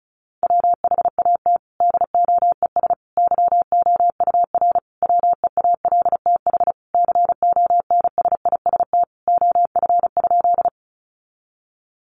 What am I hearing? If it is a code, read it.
Morse code, heard as a keyed tone, about 34 wpm, 700 Hz. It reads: WHAT DOES YOUR WEALTH CONSIST OF?
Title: Sentences from Top 1400 Words 35wpm